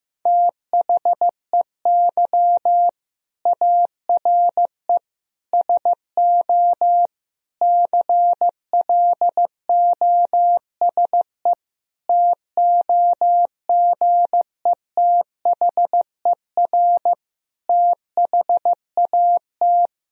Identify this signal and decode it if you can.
Morse code, heard as a keyed tone, about 15 wpm, 700 Hz. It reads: THEY ARE SO CLOSE TOGETHER THAT
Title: Sentences from Top 1000 Words - Review Part 1 - 15wpm